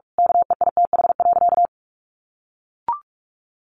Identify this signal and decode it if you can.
Morse code, heard as a keyed tone, about 44 wpm, 700 Hz. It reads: KEITH.
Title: A Sound Of Thunder 45wpm